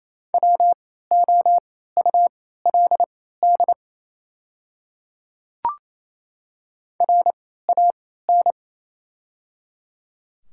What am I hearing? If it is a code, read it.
Morse code, heard as a keyed tone, about 28 wpm, spaced as for 16 wpm, 700 Hz. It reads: WOULD RAN
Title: 3x Character Spacing - Top 500 Words - 28wpm